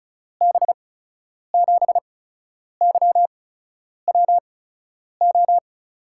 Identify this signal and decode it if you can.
Morse code, heard as a keyed tone, about 36 wpm, spaced as for 10 wpm, 700 Hz. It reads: B7YWO
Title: USA Call Signs - 6 Characters - 8x Character Spacing - 35wpm